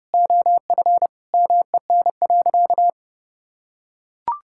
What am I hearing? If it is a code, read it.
Morse code, heard as a keyed tone, about 30 wpm, 700 Hz. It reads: OF MEN.
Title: Sentences from Top 1000 Words - Review Part 5 - 30wpm